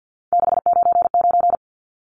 Morse code, heard as a keyed tone, about 50 wpm, 700 Hz. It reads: 699
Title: Numbers - 3-Digit 50wpm